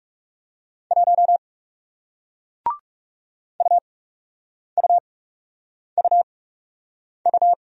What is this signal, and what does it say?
Morse code, heard as a keyed tone, about 39 wpm, spaced as for 8 wpm, 700 Hz. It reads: 1 UUUU